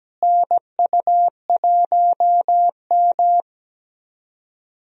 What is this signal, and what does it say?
Morse code, heard as a keyed tone, about 17 wpm, 700 Hz. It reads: NU1M
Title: USA Call Signs - 4 Characters - Repeated 3x - 17wpm